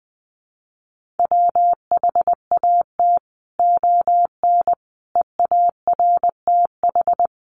Morse code, heard as a keyed tone, about 20 wpm, 700 Hz. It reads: WHAT ON EARTH
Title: Sentences from Top 1100 Words 20wpm